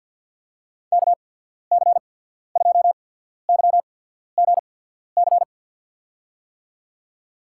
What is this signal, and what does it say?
Morse code, heard as a keyed tone, about 50 wpm, spaced as for 14 wpm, 700 Hz. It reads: KC2YGC
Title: USA Call Signs - 6 Characters - 8x Character Spacing - 50wpm